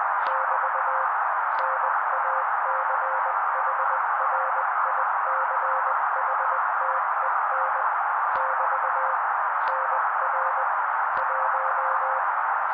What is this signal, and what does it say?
Morse code, heard as a keyed tone, about 20 wpm, 530 Hz. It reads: <BT> NACHRICHTEN <BT> NR 1
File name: daily / 18wpm noise